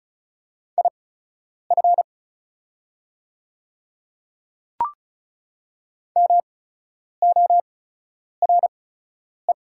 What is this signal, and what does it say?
Morse code, heard as a keyed tone, about 36 wpm, spaced as for 10 wpm, 690 Hz. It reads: IF MORE
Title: 8x Character Spacing - Top 500 Words - 35wpm